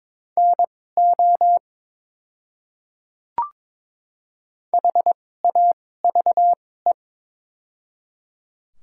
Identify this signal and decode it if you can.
Morse code, heard as a keyed tone, about 22 wpm, spaced as for 16 wpm, 690 Hz. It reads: NO HAVE
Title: 2x Character Spacing - Top 100 Words - 22wpm